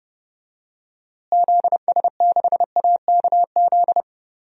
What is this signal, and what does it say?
Morse code, heard as a keyed tone, about 30 wpm, 700 Hz. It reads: ZS6AKZ
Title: DXCC-100 Call Signs - 6 Characters - Repeated 3x - 30wpm